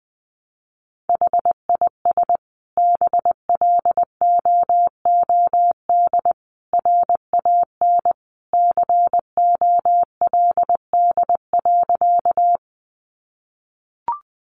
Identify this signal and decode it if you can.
Morse code, heard as a keyed tone, about 20 wpm, 700 Hz. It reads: HIS BLOOD RAN COLD.
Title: Sentences from Top 1100 Words 20wpm